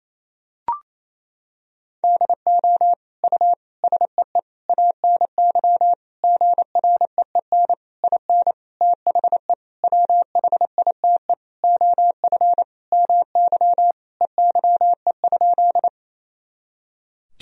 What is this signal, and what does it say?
Morse code, heard as a keyed tone, about 28 wpm, 700 Hz. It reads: DO U SEE ANY GREEN IN THE WHITE OF MY EYE?
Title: Sentences from Top 500 Words and Abbreviations 28wpm